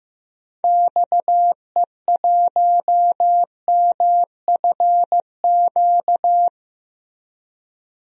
Morse code, heard as a keyed tone, about 15 wpm, 700 Hz. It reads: XE1MFQ
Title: DXCC-100 Call Signs - 6 Characters - Repeated 2x - 15wpm